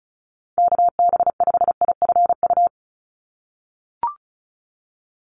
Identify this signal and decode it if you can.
Morse code, heard as a keyed tone, about 36 wpm, 700 Hz. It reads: KB5IFU
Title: USA Call Signs 35wpm